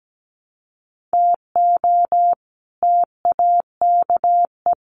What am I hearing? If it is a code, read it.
Morse code, heard as a keyed tone, about 17 wpm, 700 Hz. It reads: TO TAKE
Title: Sets of 5 Words 17wpm